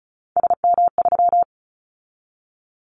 Morse code, heard as a keyed tone, about 36 wpm, 700 Hz. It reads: SM3
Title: Callsign Prefixes for Top 25 DXCC - 35wpm